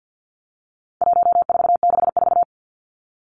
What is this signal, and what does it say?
Morse code, heard as a keyed tone, about 50 wpm, 700 Hz. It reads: 1464
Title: Numbers - 4-Digit 50wpm